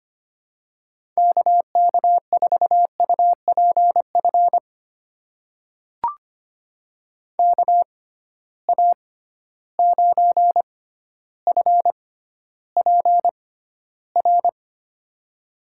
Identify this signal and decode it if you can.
Morse code, heard as a keyed tone, about 25 wpm, spaced as for 13 wpm, 700 Hz. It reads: KK4UPF KA9FPR
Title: USA Call Signs - 6 Characters - 6x Character Spacing - 25wpm